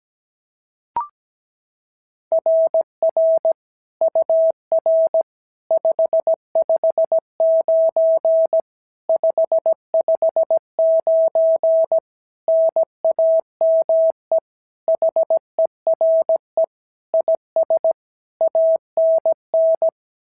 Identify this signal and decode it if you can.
Morse code, heard as a keyed tone, about 16 wpm, 650 Hz. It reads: RR UR 559 559 NAME HERE IS ANN
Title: SOTA QSO Practice 17wpm